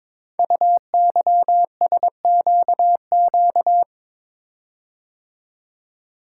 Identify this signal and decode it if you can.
Morse code, heard as a keyed tone, about 22 wpm, 700 Hz. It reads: UYSQQ
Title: Code Groups - 5 Letters - 22wpm